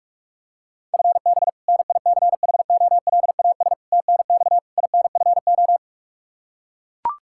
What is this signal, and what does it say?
Morse code, heard as a keyed tone, about 41 wpm, 700 Hz. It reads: FB NICHOLAS TNX INFO
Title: K1USN SST QSO Practice 45wpm